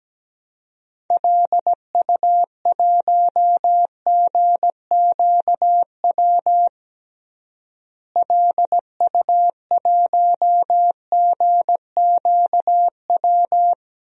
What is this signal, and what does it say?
Morse code, heard as a keyed tone, about 17 wpm, 700 Hz. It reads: LU1GQW LU1GQW
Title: DXCC-100 Call Signs - 6 Characters - Repeated 2x - 17wpm